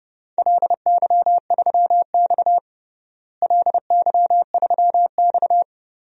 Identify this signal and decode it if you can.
Morse code, heard as a keyed tone, about 30 wpm, 700 Hz. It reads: LY3X LY3X
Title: DXCC-100 Call Signs - 4 Characters - Repeated 3x - 30wpm